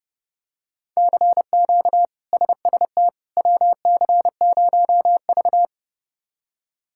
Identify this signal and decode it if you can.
Morse code, heard as a keyed tone, about 30 wpm, 700 Hz. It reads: CQ SST WC0V
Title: K1USN SST QSO Practice - 30wpm